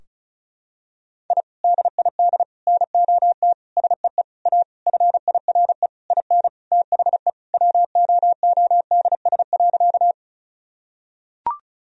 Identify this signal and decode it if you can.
Morse code, heard as a keyed tone, about 36 wpm, 700 Hz. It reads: I DID NOT SEE A FIRE IN THE WOODS.
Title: Sentences from Top 1000 Words - Review Part 2 - 35wpm